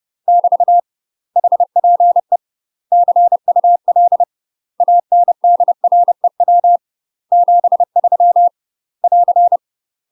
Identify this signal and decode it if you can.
Morse code, heard as a keyed tone, about 30 wpm, 700 Hz. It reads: <BT> HPE CUL ANDREW 73 <AR>
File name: UK MASTER QSOs TRAINING / 30WPM / UK MASTER QSOs 0005